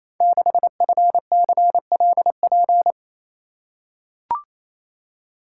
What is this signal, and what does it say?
Morse code, heard as a keyed tone, about 27 wpm, 700 Hz. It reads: 6FCLP
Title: Code Groups - 5 Letter:Numbers - 28wpm